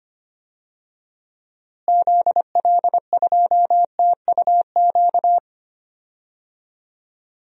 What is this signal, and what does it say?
Morse code, heard as a keyed tone, about 25 wpm, 700 Hz. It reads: ZL2TUQ
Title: DXCC-100 Call Signs - 6 Characters - Repeated 2x - 25wpm